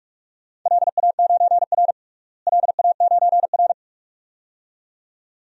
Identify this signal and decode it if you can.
Morse code, heard as a keyed tone, about 44 wpm, 700 Hz. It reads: LA9R LA9R
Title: DXCC-100 Call Signs - 4 Characters - Repeated 2x - 45wpm